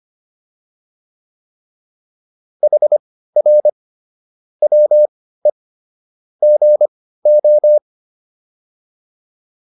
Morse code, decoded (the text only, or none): HR WE GO